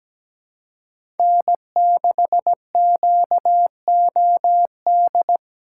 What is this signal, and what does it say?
Morse code, heard as a keyed tone, about 17 wpm, 700 Hz. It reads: N6QOD
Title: USA Call Signs - 5 Characters - 3x Character Spacing - 17wpm